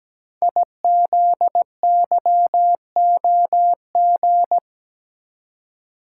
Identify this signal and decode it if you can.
Morse code, heard as a keyed tone, about 17 wpm, 700 Hz. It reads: IZYOG
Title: Code Groups - 5 Letters - 17wpm